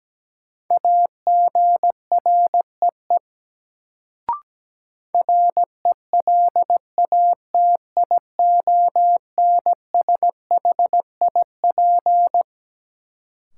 Morse code, heard as a keyed tone, about 17 wpm, 700 Hz. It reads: AGREE RELATIONSHIP
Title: Sentences from Top 1100 Words - New Word Drill Alt - 17wpm